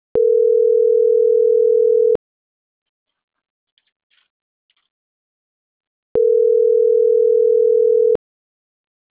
Ringback tone (North America), 440 + 480 Hz.
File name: CENTRAL MAERSK / out martin.blas